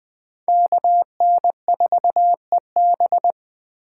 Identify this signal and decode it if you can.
Morse code, heard as a keyed tone, about 24 wpm, 700 Hz. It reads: KN4EB
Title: Speed Racing - USA 2x2 Call Signs 20wpm